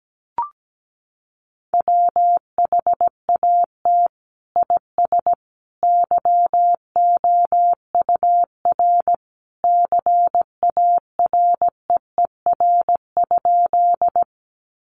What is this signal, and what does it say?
Morse code, heard as a keyed tone, about 17 wpm, 700 Hz. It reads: WHAT IS YOUR CAREER?
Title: Sentences from Top 1000 Words 17wpm